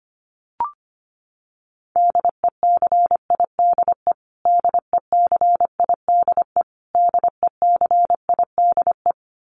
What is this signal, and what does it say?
Morse code, heard as a keyed tone, about 25 wpm, 700 Hz. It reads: DECIDE DECIDE DECIDE
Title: Top 500 Words Repeated 25wpm